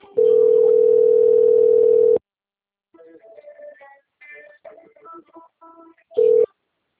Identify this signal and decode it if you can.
Ringback tone (North America), 440 + 480 Hz.